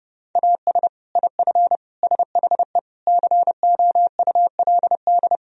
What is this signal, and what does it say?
Morse code, heard as a keyed tone, about 32 wpm, 700 Hz. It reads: AS IF SHE COULD
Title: Sentences from Top 1000 Words - Review Part 1 - 30wpm